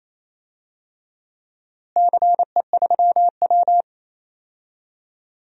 Morse code, heard as a keyed tone, about 28 wpm, 700 Hz. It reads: CE3W
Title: DXCC-100 Call Signs - 4 Characters - Repeated 3x - 28wpm